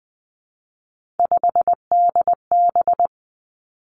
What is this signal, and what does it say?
Morse code, heard as a keyed tone, about 20 wpm, 700 Hz. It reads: 5DB
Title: Three Challenging Letter-Number Combinations 20wpm